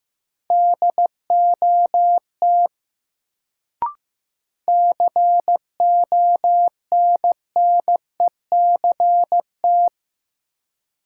Morse code, heard as a keyed tone, about 15 wpm, 700 Hz. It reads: DOT CONNECT
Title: Sentences from Top 1500 Words - New Word Drill Alt - 15wpm